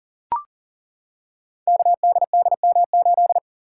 Morse code, heard as a keyed tone, about 38 wpm, 700 Hz. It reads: KDDM8
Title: Code Groups - 5 Letter:Numbers - 40wpm